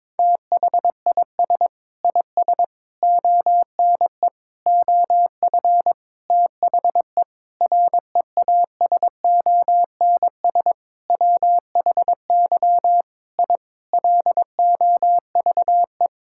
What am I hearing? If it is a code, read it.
Morse code, heard as a keyed tone, about 22 wpm, 700 Hz. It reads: THIS IS ONE OF THE REASONS WHY I LOVE
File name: Sentences with 7-Word Phrases - 22wpm